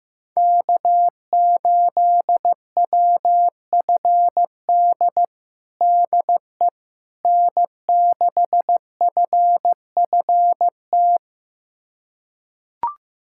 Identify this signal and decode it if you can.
Morse code, heard as a keyed tone, about 16 wpm, 700 Hz. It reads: K8WFD DE N6FFT
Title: USA Callsign DE Callsign - 5 Characters - 15wpm